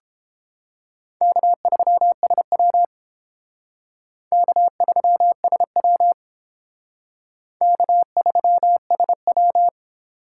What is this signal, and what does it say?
Morse code, heard as a keyed tone, about 30 wpm, 700 Hz. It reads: K3SW K3SW K3SW